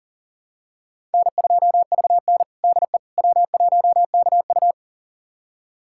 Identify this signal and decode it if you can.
Morse code, heard as a keyed tone, about 40 wpm, 700 Hz. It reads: N2VN DE W1KU
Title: USA Callsign DE Callsign 40wpm